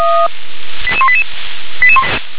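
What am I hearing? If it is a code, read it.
Five-tone selective call (ZVEI), heard as quick runs of five tones, tones of about 70 ms. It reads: 88188
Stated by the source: ZVEI five-tone selective call received on an SDR.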